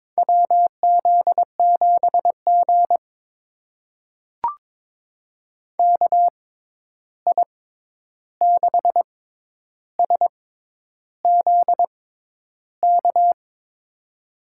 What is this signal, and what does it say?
Morse code, heard as a keyed tone, about 22 wpm, spaced as for 10 wpm, 700 Hz. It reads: WZ7G KI6SZK
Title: USA Call Signs - 6x Character Spacing - 22wpm